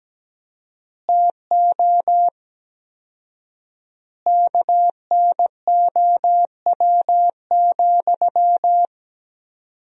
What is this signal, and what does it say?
Morse code, heard as a keyed tone, about 17 wpm, 700 Hz. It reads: TO KNOW,
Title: 4x Word-Spacing - Sentences from Top 200 Words - 17wpm